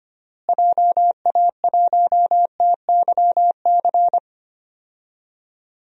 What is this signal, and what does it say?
Morse code, heard as a keyed tone, about 25 wpm, 700 Hz. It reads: JA1TYC
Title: DXCC-100 Call Signs - 6 Characters - Repeated 2x - 25wpm